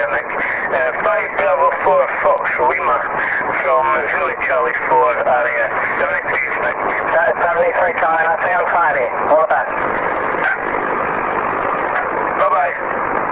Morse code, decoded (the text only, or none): none